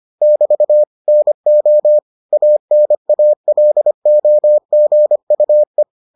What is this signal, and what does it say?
Morse code, heard as a keyed tone, about 25 wpm, 600 Hz. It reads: <BT> NO ANALOGUE